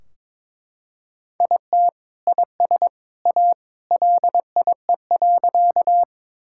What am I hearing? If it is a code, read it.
Morse code, heard as a keyed tone, about 23 wpm, 700 Hz. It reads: IT IS A LIE.
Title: Sentences from Top 1600 Words 22wpm